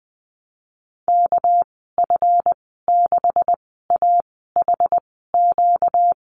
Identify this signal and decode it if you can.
Morse code, heard as a keyed tone, about 20 wpm, spaced as for 14 wpm, 700 Hz. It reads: KF6AHQ
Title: 245 Lesson 39g - Letter z - Call Signs 20wpm